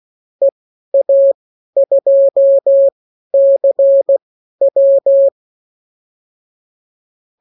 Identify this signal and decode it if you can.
Morse code, heard as a keyed tone, about 16 wpm, spaced as for 12 wpm, 550 Hz. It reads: EA2CW
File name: CWconcurso QSO libroCW4 16-8ppm